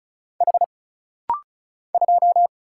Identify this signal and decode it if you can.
Morse code, heard as a keyed tone, about 34 wpm, 700 Hz. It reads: H 2
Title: Single Letter-Number Alt - Rapid-Fire 35wpm